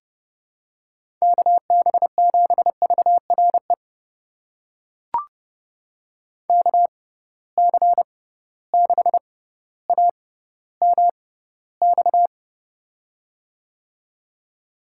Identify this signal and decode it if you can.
Morse code, heard as a keyed tone, about 32 wpm, spaced as for 16 wpm, 700 Hz. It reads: KB7VRE KC6AMX
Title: USA Call Signs - 6 Characters - 6x Character Spacing - 30wpm